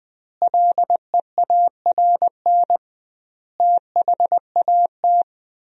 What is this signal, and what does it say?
Morse code, heard as a keyed tone, about 20 wpm, 700 Hz. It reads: LEARN THAT
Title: 2x Word-Spacing - Sentences from Top 300 Words - 20wpm